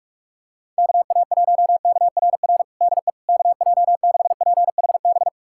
Morse code, heard as a keyed tone, about 46 wpm, 700 Hz. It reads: KA1KRR DE KJ6PHB
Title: USA Callsign DE Callsign - 6 Characters - 45wpm